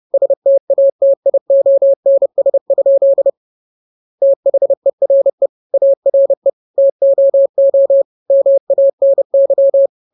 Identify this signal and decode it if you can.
Morse code, heard as a keyed tone, about 30 wpm, 550 Hz. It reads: STATIONS? THERE ARE TOO MANY